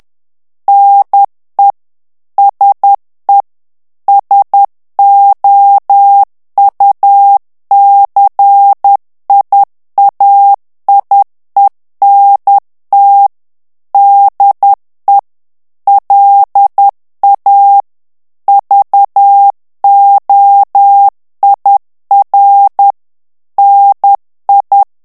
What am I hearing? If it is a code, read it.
Morse code, heard as a keyed tone, about 10 wpm, 800 Hz. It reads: NE SE SOUCIAIENT DE LA VOIR NI
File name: CW / FICHIERS-MP3 / FRANCAIS TEXTE CLAIR 2 VITESSE 10 MOTS MINUTES